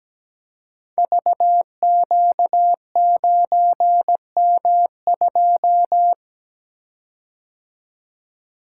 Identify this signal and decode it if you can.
Morse code, heard as a keyed tone, about 17 wpm, 700 Hz. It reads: VQ9M2
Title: Code Groups - 5 Letter:Numbers - 17wpm